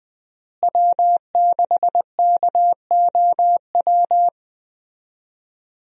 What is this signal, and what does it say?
Morse code, heard as a keyed tone, about 20 wpm, 700 Hz. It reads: W6KOW
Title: USA Call Signs - 5 Characters - Repeated 2x - 20wpm